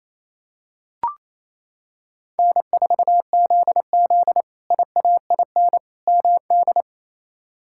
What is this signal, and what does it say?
Morse code, heard as a keyed tone, about 28 wpm, 700 Hz. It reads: N4ZZ IAIN MD
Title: K1USN SST QSO Practice - 28wpm